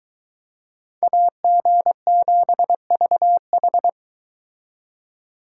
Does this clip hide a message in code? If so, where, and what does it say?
Yes: Morse code from 1.0 to 3.9 s, heard as a keyed tone, about 23 wpm, 700 Hz. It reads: AG7VH